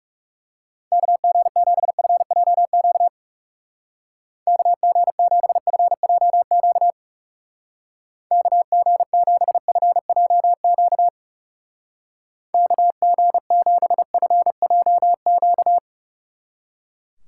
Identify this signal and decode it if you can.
Morse code, heard as a keyed tone, about 38 wpm, 700 Hz. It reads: KG7FJQ KG7FJQ KG7FJQ KG7FJQ